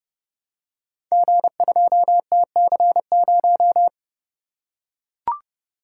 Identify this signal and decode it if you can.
Morse code, heard as a keyed tone, about 30 wpm, 700 Hz. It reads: G2TC0
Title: Code Groups - 5 Letter:Numbers - 30wpm